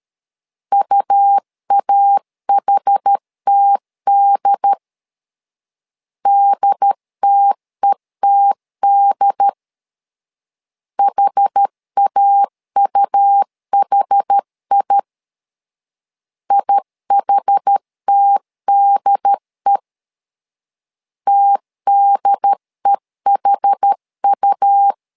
Morse code, heard as a keyed tone, about 13 wpm, 800 Hz. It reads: UAHTD DTETD HAUHI IHTDE TDEHU
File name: LEZIONI / 2 UHDN / 2 UHD 12wpm 1540